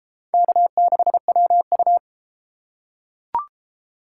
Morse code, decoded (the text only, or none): K6WU